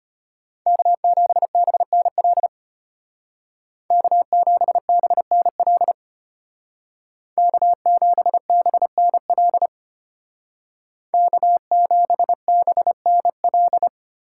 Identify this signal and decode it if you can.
Morse code, heard as a keyed tone, about 32 wpm, 700 Hz. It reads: K7BNL K7BNL K7BNL K7BNL